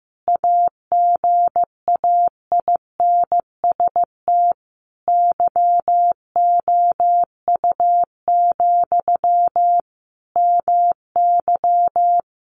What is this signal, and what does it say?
Morse code, heard as a keyed tone, about 15 wpm, 700 Hz. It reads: AGAINST YOU, MY
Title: Sentences from Top 700 Words 15wpm